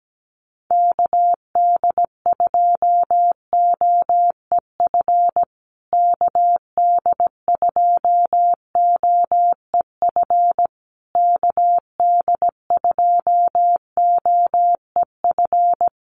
Morse code, heard as a keyed tone, about 17 wpm, 700 Hz. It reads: KD2OEF KD2OEF KD2OEF
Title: USA Call Signs Repeated 3x 17wpm